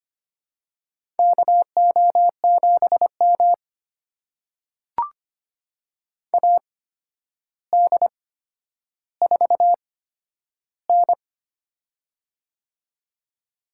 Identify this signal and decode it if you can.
Morse code, heard as a keyed tone, about 25 wpm, spaced as for 11 wpm, 700 Hz. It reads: KO7M AD4N
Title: USA Call Signs - 4 Characters - 8x Character Spacing - 25wpm